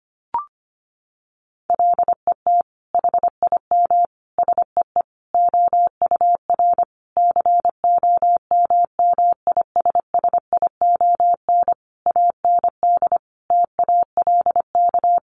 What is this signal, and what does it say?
Morse code, heard as a keyed tone, about 25 wpm, 700 Hz. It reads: LET HIM SEE OUR COMMISSION AND TALK
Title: Sentences from Top 1400 Words 25wpm